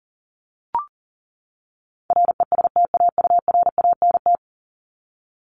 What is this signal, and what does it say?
Morse code, heard as a keyed tone, about 40 wpm, 700 Hz. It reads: RESTAURANT
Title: Top 1751-2000 Words 40wpm